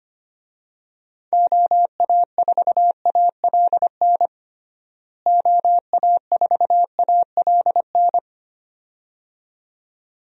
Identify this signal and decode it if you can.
Morse code, heard as a keyed tone, about 25 wpm, 700 Hz. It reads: OA4ALN OA4ALN
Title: USA Call Signs - 6 Characters - Repeated 2x - 25wpm